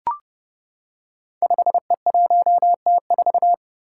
Morse code, heard as a keyed tone, about 30 wpm, 700 Hz. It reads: 5E1T4
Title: Five Challenging Letter-Number Combinations 30wpm